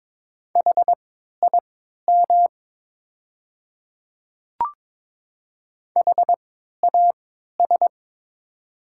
Morse code, heard as a keyed tone, about 22 wpm, spaced as for 12 wpm, 690 Hz. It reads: HIM HAS
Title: 3x Character Spacing - Top 100 Words - 22wpm